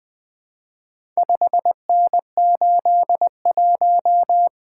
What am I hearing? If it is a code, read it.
Morse code, heard as a keyed tone, about 20 wpm, 700 Hz. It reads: 5N81
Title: Four Letter-Number Combinations - 20wpm